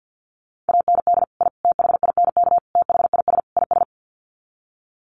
Morse code, heard as a keyed tone, about 50 wpm, 700 Hz. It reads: AND I THINK THIS IS
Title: Sets of 5 Words Encore 50wpm